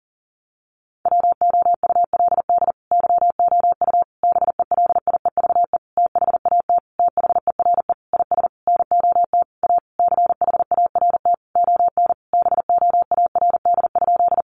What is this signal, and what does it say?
Morse code, heard as a keyed tone, about 40 wpm, 700 Hz. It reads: WOULD YOU BELIEVE THAT THERE IS NOT A CHART ON BOARD?
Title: Sentences from Top 1300 Words 40wpm